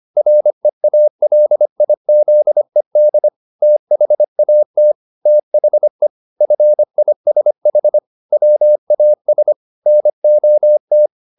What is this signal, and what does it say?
Morse code, heard as a keyed tone, about 25 wpm, 600 Hz. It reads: REALIZED THAT THE FISH WAS NOT
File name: the old man and the sea part 1 25wpm